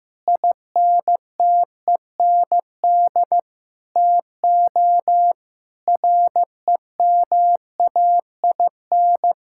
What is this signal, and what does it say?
Morse code, heard as a keyed tone, about 15 wpm, 700 Hz. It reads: INTEND TO REMAIN
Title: Sentences from Top 1600 Words 15wpm